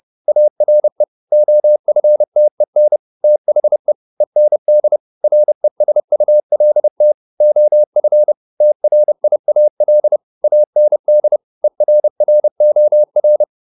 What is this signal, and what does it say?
Morse code, heard as a keyed tone, about 30 wpm, 600 Hz. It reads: ARE OFTEN THE END RESULT OF TRIAL AND ERROR